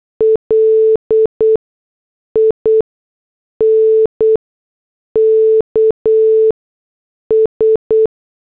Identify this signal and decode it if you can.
Morse code, heard as a keyed tone, about 8 wpm, 440 Hz. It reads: L I N K S
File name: morse / headlines